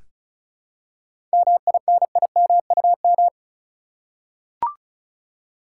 Morse code, heard as a keyed tone, about 36 wpm, 700 Hz. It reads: MINIMUM
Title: Seven Letter Words - 35wpm